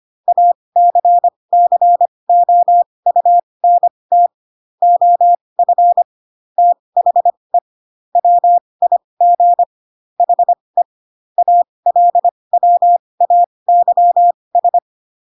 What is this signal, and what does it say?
Morse code, heard as a keyed tone, about 25 wpm, spaced as for 20 wpm, 700 Hz. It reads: ACCOUNT OF THE WIG HE ALWAYS
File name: Pinocchio / Pinocchio ch2 15 25 ik1hge